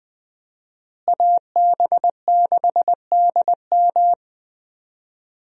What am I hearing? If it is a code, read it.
Morse code, heard as a keyed tone, about 20 wpm, 700 Hz. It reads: AB6DM
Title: USA Call Signs - 5 Characters - Repeated 2x - 20wpm